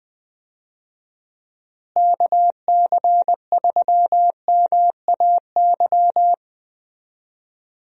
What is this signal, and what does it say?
Morse code, heard as a keyed tone, about 20 wpm, 700 Hz. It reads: KC3MAY
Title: USA Call Signs - 6 Characters - Repeated 3x - 20wpm